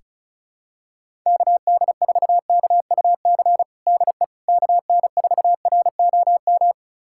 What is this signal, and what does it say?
Morse code, heard as a keyed tone, about 36 wpm, 700 Hz. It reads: KD4KUC DE KN4ROM
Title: USA Callsign DE Callsign 35wpm